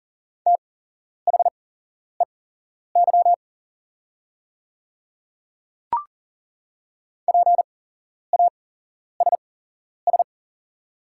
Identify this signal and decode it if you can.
Morse code, heard as a keyed tone, about 40 wpm, spaced as for 11 wpm, 690 Hz. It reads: THEY PASS
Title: 8x Character Spacing - Top 500 Words - 40wpm